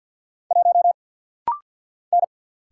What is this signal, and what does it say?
Morse code, heard as a keyed tone, about 49 wpm, 700 Hz. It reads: 1 N
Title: Single Letter-Number Alt - Rapid-Fire 50wpm